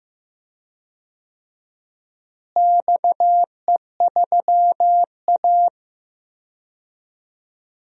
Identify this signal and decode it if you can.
Morse code, heard as a keyed tone, about 15 wpm, 700 Hz. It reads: XE3A
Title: DXCC-100 Call Signs - 4 Characters - Repeated 2x - 15wpm